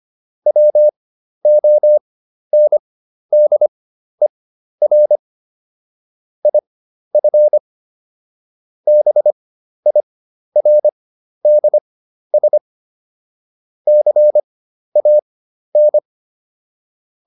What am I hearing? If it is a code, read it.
Morse code, heard as a keyed tone, about 25 wpm, spaced as for 12 wpm, 600 Hz. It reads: WONDER IF BIRDS CAN